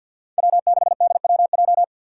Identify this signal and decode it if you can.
Morse code, heard as a keyed tone, about 52 wpm, 700 Hz. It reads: W6DWJ
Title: Five Challenging Letter-Number Combinations 50wpm